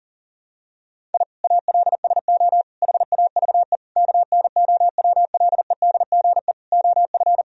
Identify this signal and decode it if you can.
Morse code, heard as a keyed tone, about 40 wpm, 700 Hz. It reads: I ALSO HAVE KNOWLEDGE OF
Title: Sentences from Top 1000 Words - Review Part 3 - 40wpm